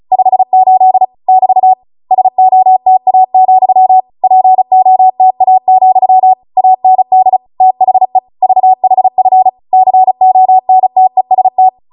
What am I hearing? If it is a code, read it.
Morse code, heard as a keyed tone, about 36 wpm, 750 Hz. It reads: 58 <BT> SOTA, POTA, AND THE VHF CONTEST